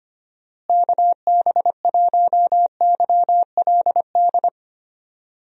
Morse code, heard as a keyed tone, about 25 wpm, 700 Hz. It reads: KB1YLD
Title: USA Call Signs - 6 Characters - Repeated 2x - 25wpm